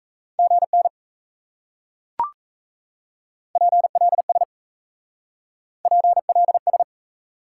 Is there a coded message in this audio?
Yes: Morse code from 0.4 to 6.8 s, heard as a keyed tone, about 37 wpm, 700 Hz. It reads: GN PLS PLS